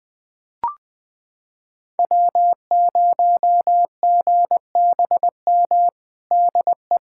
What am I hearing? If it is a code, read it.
Morse code, heard as a keyed tone, about 20 wpm, 700 Hz. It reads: W0GBM DE
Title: USA Callsign DE Callsign - 5 Characters - 20wpm